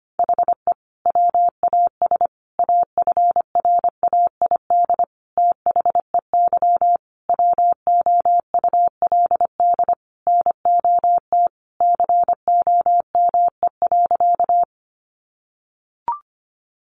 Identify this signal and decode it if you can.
Morse code, heard as a keyed tone, about 25 wpm, 700 Hz. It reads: HE WAS AFRAID THEY WOULD NOT COME.
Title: Sentences from Top 1500 Words 25wpm